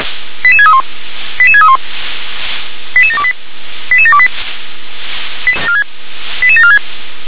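Five-tone selective call (ZVEI), heard as quick runs of five tones, tones of about 70 ms. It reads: 88631 88631 88728 88728 88746
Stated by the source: Five-tone ZVEI selective call received on an SDR.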